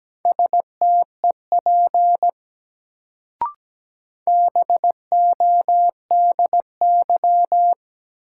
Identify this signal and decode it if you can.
Morse code, heard as a keyed tone, about 17 wpm, 700 Hz. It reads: STEP BODY
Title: Sentences from Top 400 Words - New Word Drill - 17wpm